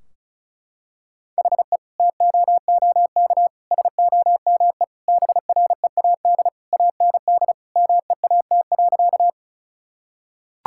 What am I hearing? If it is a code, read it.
Morse code, heard as a keyed tone, about 34 wpm, 700 Hz. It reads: HE TOOK SOME BREAD AND MEAT.
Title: Sentences from Top 1600 Words 35wpm